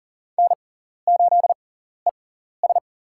Morse code, heard as a keyed tone, about 39 wpm, spaced as for 14 wpm, 700 Hz. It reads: N8ES
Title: USA Call Signs - 4 Characters - 6x Character Spacing - 40wpm